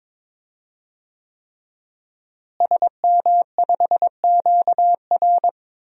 Morse code, heard as a keyed tone, about 21 wpm, 700 Hz. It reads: SM5QR
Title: DXCC-100 Call Signs - 5 Characters - Repeated 3x - 22wpm